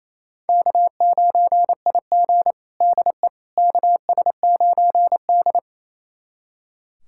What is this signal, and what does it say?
Morse code, heard as a keyed tone, about 28 wpm, 700 Hz. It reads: K9IG DE KS9D
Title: USA Callsign DE Callsign - 4 Characters - 28wpm